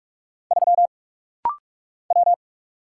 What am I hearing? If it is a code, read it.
Morse code, heard as a keyed tone, about 44 wpm, 700 Hz. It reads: 3 W
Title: Single Letter-Number - Rapid-Fire 45wpm